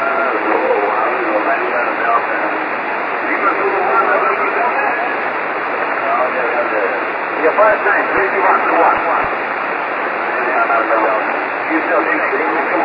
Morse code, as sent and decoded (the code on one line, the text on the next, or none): none